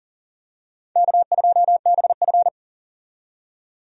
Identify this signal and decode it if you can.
Morse code, heard as a keyed tone, about 40 wpm, 700 Hz. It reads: K2BF
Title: USA Call Signs - 4 Characters - 40wpm